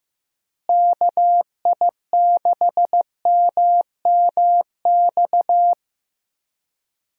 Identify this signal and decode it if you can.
Morse code, heard as a keyed tone, about 15 wpm, 700 Hz. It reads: KI6MMX
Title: USA Call Signs - 6 Characters - Repeated 3x - 15wpm